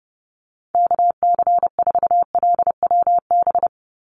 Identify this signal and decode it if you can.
Morse code, heard as a keyed tone, about 30 wpm, 700 Hz. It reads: KC4LWB
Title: USA Call Signs 30wpm